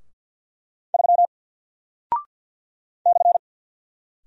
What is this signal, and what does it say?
Morse code, heard as a keyed tone, about 50 wpm, 700 Hz. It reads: 3 /
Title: Single Character (Letter:Number:Punctuation:Prosign) 50wpm